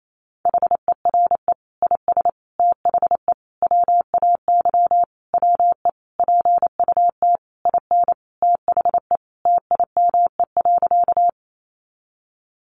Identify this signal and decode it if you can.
Morse code, heard as a keyed tone, about 28 wpm, 700 Hz. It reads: HERE IS THE WAY WE PUT IN THE TIME.
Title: Sentences from Top 200 Words 28wpm